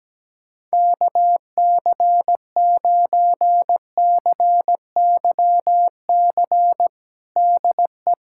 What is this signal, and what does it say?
Morse code, heard as a keyed tone, about 17 wpm, 700 Hz. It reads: KC9CYC DE
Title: USA Callsign DE Callsign 17wpm